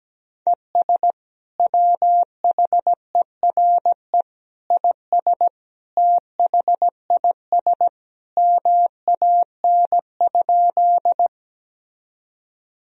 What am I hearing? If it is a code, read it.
Morse code, heard as a keyed tone, about 17 wpm, 700 Hz. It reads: ES WHERE IS THIS MAN?
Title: Sentences from Top 200 Words and Abbreviations 17wpm